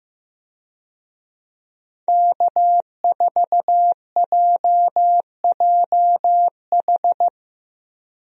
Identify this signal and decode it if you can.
Morse code, heard as a keyed tone, about 16 wpm, 700 Hz. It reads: K4JJH
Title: USA Call Signs - 5 Characters - Repeated 2x - 15wpm